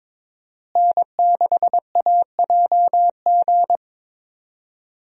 Morse code, heard as a keyed tone, about 22 wpm, 700 Hz. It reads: N6AJG
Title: USA Call Signs - 5 Characters - Repeated 2x - 22wpm